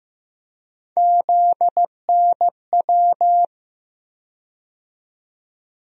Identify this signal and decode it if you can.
Morse code, heard as a keyed tone, about 15 wpm, 700 Hz. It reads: ZNW
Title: USA Call Sign Postfixes - 3 Letters 15wpm